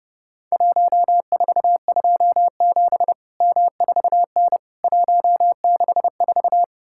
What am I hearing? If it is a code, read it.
Morse code, heard as a keyed tone, about 30 wpm, 700 Hz. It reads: 1427 M4N 164
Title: Worked All Europe DX Contest - QTC Exchange - 30wpm